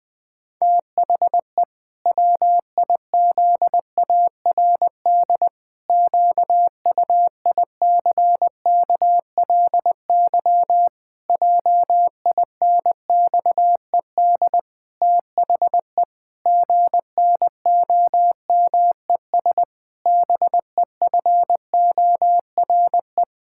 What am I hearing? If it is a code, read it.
Morse code, heard as a keyed tone, about 21 wpm, 700 Hz. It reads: THE WIZARD QUICKLY JINXED THE GNOMES BEFORE
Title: Pangram Challenge v2 - 20wpm